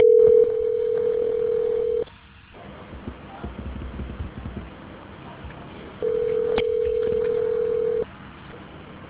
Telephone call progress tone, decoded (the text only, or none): ringback tone